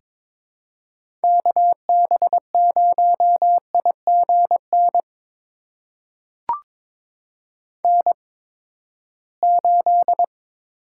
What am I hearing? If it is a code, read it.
Morse code, heard as a keyed tone, about 22 wpm, spaced as for 15 wpm, 700 Hz. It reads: KB0IGN N8
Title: USA Call Signs - 8x Character Spacing - 22wpm